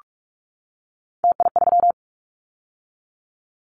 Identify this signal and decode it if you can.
Morse code, heard as a keyed tone, about 46 wpm, 700 Hz. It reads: TI3
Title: Callsign Prefixes for Top 100 DXCC - 45wpm